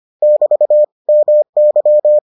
Morse code, heard as a keyed tone, about 25 wpm, 600 Hz. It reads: <BT> MY